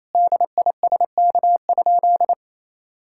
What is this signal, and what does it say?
Morse code, heard as a keyed tone, about 27 wpm, 700 Hz. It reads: DISK?
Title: Sentences from Top 1000 Words - Review Part 6 - 28wpm